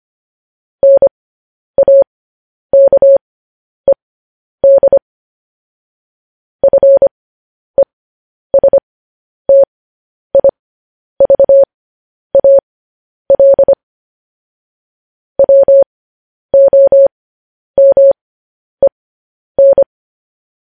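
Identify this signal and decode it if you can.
Morse code, heard as a keyed tone, about 25 wpm, spaced as for 10 wpm, 550 Hz. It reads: NAKED FESTIVAL WOMEN
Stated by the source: Morse code news headlines practice transmission.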